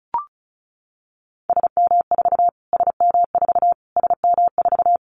Morse code, heard as a keyed tone, about 34 wpm, 700 Hz. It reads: SM4 SM4 SM4
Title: Callsign Prefixes for Top 25 DXCC - Repeated 3x - 35wpm